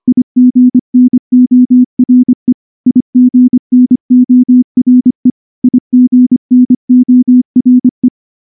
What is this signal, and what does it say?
Morse code, heard as a keyed tone, about 23 wpm, 250 Hz. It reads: IGNORE IGNORE IGNORE
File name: morse / morse-training-25WPM